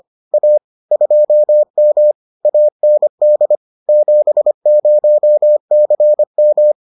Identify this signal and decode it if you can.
Morse code, heard as a keyed tone, about 25 wpm, 600 Hz. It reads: A 2M AND 70CM